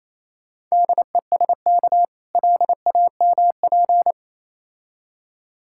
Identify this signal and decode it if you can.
Morse code, heard as a keyed tone, about 27 wpm, 700 Hz. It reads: DESK LAMP
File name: Easy Words - Speed-Racing with Hints - 22wpm